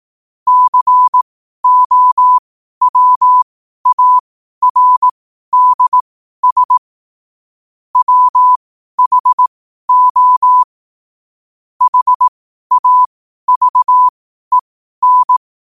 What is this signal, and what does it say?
Morse code, heard as a keyed tone, about 18 wpm, spaced as for 12 wpm, 1000 Hz. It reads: COWARDS WHO HAVEN
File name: qotdfiles / 12wpm